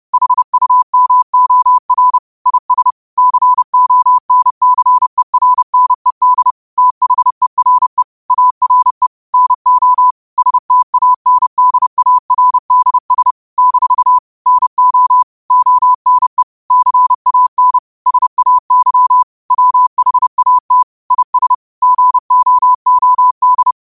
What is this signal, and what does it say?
Morse code, heard as a keyed tone, about 30 wpm, 1000 Hz. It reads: HUMOR IS CONCERNED THERE ARE NO STANDARDS - NO ONE CAN SAY WHAT IS GOOD
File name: qotdfiles / 30wpm